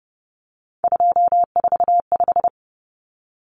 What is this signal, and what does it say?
Morse code, heard as a keyed tone, about 30 wpm, 700 Hz. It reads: 245
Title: Common Cut Numbers - 3-Digit 30wpm